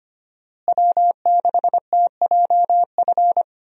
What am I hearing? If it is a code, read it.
Morse code, heard as a keyed tone, about 25 wpm, 700 Hz. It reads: W6TJF
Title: USA Call Signs - 5 Characters - Repeated 3x - 25wpm